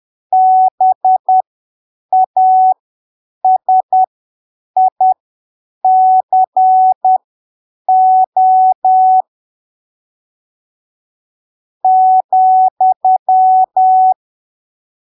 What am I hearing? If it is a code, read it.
Morse code, heard as a keyed tone, about 10 wpm, spaced as for 7 wpm, 750 Hz. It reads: BASICO ,
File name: arquivos / texto / texto 09 10ppm